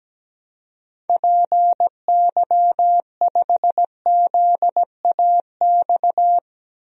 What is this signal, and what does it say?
Morse code, heard as a keyed tone, about 17 wpm, 700 Hz. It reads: PY5ZAX
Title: DXCC-100 Call Signs - 6 Characters - 17wpm